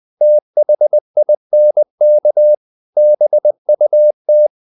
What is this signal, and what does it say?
Morse code, heard as a keyed tone, about 20 wpm, 600 Hz. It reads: THINK BUT
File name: the old man and the sea / the old man and the sea part 1 20wpm